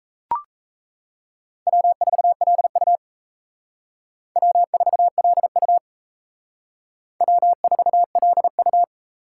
Morse code, heard as a keyed tone, about 35 wpm, 700 Hz. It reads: W4LU W4LU W4LU
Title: Speed Racing - USA 1x2 Call Signs 28wpm